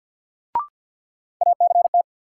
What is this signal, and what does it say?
Morse code, heard as a keyed tone, about 53 wpm, 700 Hz. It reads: ACT
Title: Sentences from Top 200 Words - New Word Drill - 50wpm